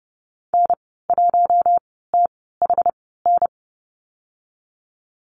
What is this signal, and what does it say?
Morse code, heard as a keyed tone, about 30 wpm, spaced as for 17 wpm, 700 Hz. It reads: N1THN
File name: Lesson 14g - Number 2 - Call Signs 30wpm